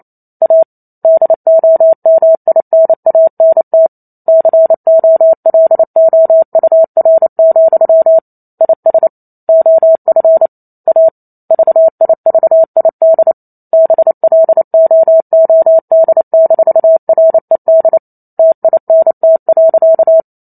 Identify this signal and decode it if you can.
Morse code, heard as a keyed tone, about 28 wpm, 650 Hz. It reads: A DOMINANT COLOUR, IS OF A VIVID BLOOD-RED TINT.